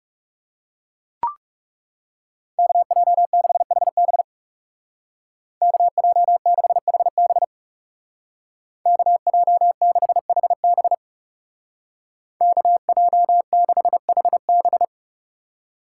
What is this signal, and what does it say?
Morse code, heard as a keyed tone, about 38 wpm, 700 Hz. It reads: KJ6HB KJ6HB KJ6HB KJ6HB